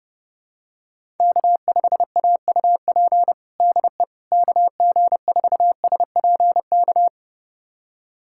Morse code, heard as a keyed tone, about 30 wpm, 700 Hz. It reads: K5AUP DE KG4SPK
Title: USA Callsign DE Callsign 30wpm